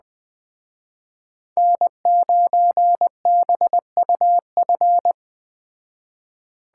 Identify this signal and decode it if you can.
Morse code, heard as a keyed tone, about 20 wpm, 700 Hz. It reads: N9BUF
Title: Speed Racing - USA 1x3 Call Signs 17wpm